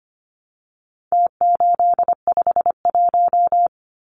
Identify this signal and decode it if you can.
Morse code, heard as a keyed tone, about 25 wpm, 700 Hz. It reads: T851
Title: Numbers - Contest Style - 4-Digit 25wpm